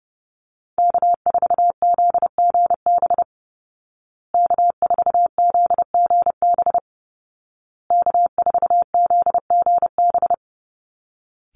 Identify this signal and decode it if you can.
Morse code, heard as a keyed tone, about 28 wpm, 700 Hz. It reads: K4ZGB K4ZGB K4ZGB
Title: Call Signs Repeated 2x - 30wpm